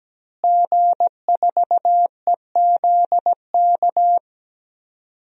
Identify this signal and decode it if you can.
Morse code, heard as a keyed tone, about 17 wpm, 700 Hz. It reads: G4EZK
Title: DXCC-100 Call Signs - 5 Characters - Repeated 2x - 17wpm